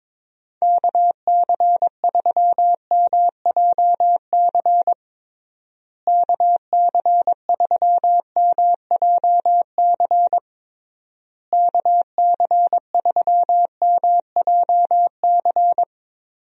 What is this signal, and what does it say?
Morse code, heard as a keyed tone, about 22 wpm, 700 Hz. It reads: KC3MJC KC3MJC KC3MJC
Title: USA Call Signs - 6 Characters - Repeated 3x - 22wpm